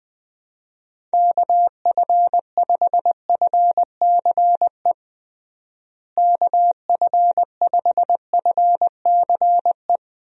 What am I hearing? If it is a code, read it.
Morse code, heard as a keyed tone, about 20 wpm, 700 Hz. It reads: KF5FCE KF5FCE
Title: USA Call Signs - 6 Characters - Repeated 2x - 20wpm